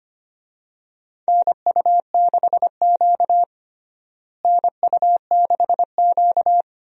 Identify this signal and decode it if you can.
Morse code, heard as a keyed tone, about 25 wpm, 700 Hz. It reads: NU6Q NU6Q
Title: USA Call Signs - 4 Characters - Repeated 3x - 25wpm